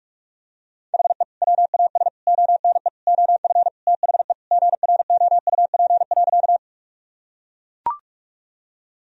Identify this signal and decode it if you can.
Morse code, heard as a keyed tone, about 42 wpm, 700 Hz. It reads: HE WAS ONE OF THE GROUP.
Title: Sentences from Top 1000 Words - Review Part 1 - 45wpm